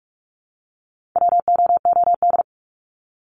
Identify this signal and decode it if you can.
Morse code, heard as a keyed tone, about 44 wpm, 700 Hz. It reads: WOOD
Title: Top 300 Words 45wpm